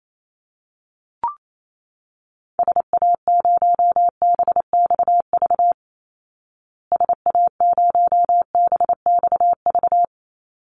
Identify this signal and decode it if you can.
Morse code, heard as a keyed tone, about 28 wpm, 700 Hz. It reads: SA0BXV SA0BXV
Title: Call Signs Repeated 3x - 28wpm